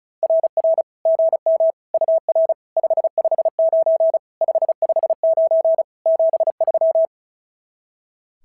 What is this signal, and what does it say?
Morse code, heard as a keyed tone, about 34 wpm, 660 Hz. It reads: RR GM UR 559 559 73
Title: SOTA QSO Practice 35wpm